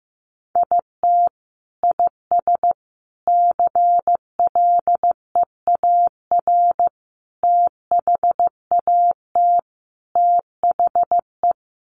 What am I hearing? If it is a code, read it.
Morse code, heard as a keyed tone, about 15 wpm, 700 Hz. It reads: IT IS CLEAR THAT THE
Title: Sets of 5 Words Encore 15wpm